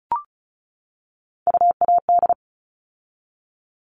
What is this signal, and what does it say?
Morse code, heard as a keyed tone, about 34 wpm, 700 Hz. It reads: UAD
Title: All Cut Numbers - 3-Digit 35wpm